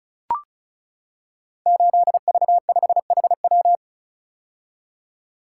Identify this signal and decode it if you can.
Morse code, heard as a keyed tone, about 36 wpm, 700 Hz. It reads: 8V5HW
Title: Code Groups - 5 Letter:Numbers - 35wpm